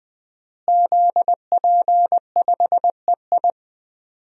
Morse code, heard as a keyed tone, about 20 wpm, 700 Hz. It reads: ZP5EI
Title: DXCC-100 Call Signs - 5 Characters - Repeated 2x - 20wpm